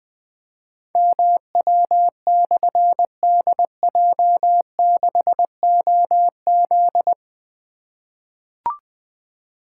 Morse code, heard as a keyed tone, about 20 wpm, 700 Hz. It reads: MW/DJ6OZ
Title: Difficult Call Signs 20wpm